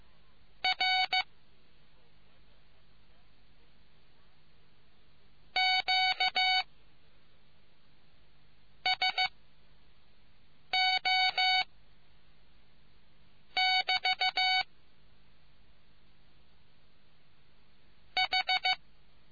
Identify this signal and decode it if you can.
Morse code, heard as a keyed tone, about 15 wpm, 750 Hz. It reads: R Q S O <BT> H